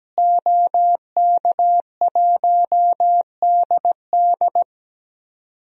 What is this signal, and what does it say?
Morse code, heard as a keyed tone, about 17 wpm, 700 Hz. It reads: OK1DD
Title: DXCC-100 Call Signs - 5 Characters - Repeated 2x - 17wpm